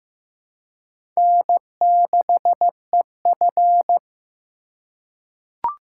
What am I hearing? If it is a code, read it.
Morse code, heard as a keyed tone, about 15 wpm, 700 Hz. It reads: N6EF
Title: USA Call Signs - 4 Characters - 3x Character Spacing - 15wpm